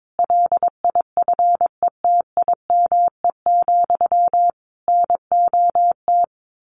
Morse code, heard as a keyed tone, about 22 wpm, 700 Hz. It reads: LIFETIME, NOT
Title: A Sound Of Thunder 22wpm